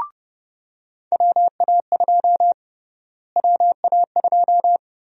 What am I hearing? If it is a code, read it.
Morse code, heard as a keyed tone, about 32 wpm, 700 Hz. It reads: WA2 WA2
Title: USA Call Sign Prefixes - Repeated 2x - 30wpm